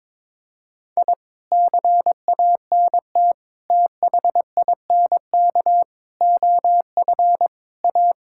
Morse code, heard as a keyed tone, about 22 wpm, 700 Hz. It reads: I CANT THINK OF A
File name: practice / 22wpm / Sets of 8 Words - 22wpm